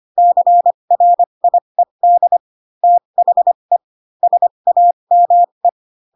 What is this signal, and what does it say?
Morse code, heard as a keyed tone, about 25 wpm, spaced as for 22 wpm, 700 Hz. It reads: CRIED THE SAME